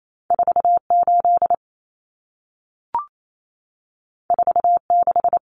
Morse code, heard as a keyed tone, about 28 wpm, 700 Hz. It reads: 48 46
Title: Common Cut Numbers - 2-Digit 28wpm